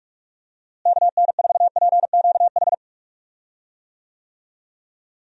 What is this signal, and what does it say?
Morse code, heard as a keyed tone, about 45 wpm, 700 Hz. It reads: KN4PQH
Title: USA Call Signs - 6 Characters - Repeated 2x - 45wpm